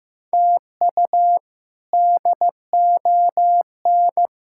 Morse code, heard as a keyed tone, about 15 wpm, 700 Hz. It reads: TU DON
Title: K1USN SST QSO Practice - 15wpm